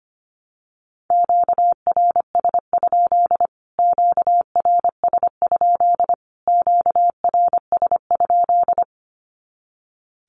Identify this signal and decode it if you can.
Morse code, heard as a keyed tone, about 25 wpm, 700 Hz. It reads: QRS? QRS? QRS?
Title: Q-Codes Repeated 3x 25wpm